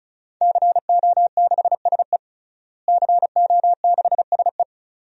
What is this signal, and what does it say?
Morse code, heard as a keyed tone, about 36 wpm, 700 Hz. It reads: CO6SE CO6SE
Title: DXCC-100 Call Signs - 5 Characters - Repeated 2x - 35wpm